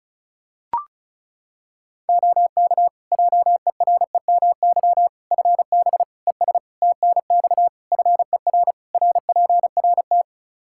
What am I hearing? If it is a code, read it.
Morse code, heard as a keyed tone, about 36 wpm, 700 Hz. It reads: OK JEREMY FB ES TNX FER RPRT
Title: Practice QSO - Exchange 2 - Info - 35wpm